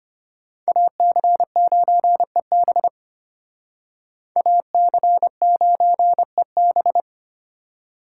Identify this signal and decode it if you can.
Morse code, heard as a keyed tone, about 33 wpm, 700 Hz. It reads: AC9EB AC9EB